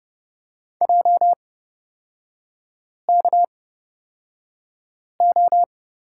Morse code, heard as a keyed tone, about 30 wpm, spaced as for 5 wpm, 700 Hz. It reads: JKO
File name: Lesson 37e - Letter x - Three Characters 30wpm